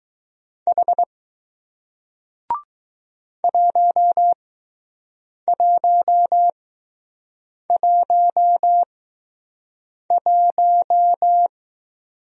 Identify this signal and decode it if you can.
Morse code, heard as a keyed tone, about 20 wpm, spaced as for 6 wpm, 700 Hz. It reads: H 1111